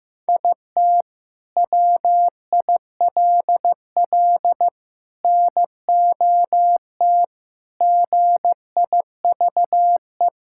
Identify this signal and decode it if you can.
Morse code, heard as a keyed tone, about 15 wpm, 700 Hz. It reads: IT WILL NOT GIVE
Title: Sentences from Top 1600 Words 15wpm